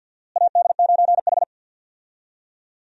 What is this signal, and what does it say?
Morse code, heard as a keyed tone, about 52 wpm, 700 Hz. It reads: AD9H
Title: USA Call Signs - 4 Characters - Repeated 3x - 50wpm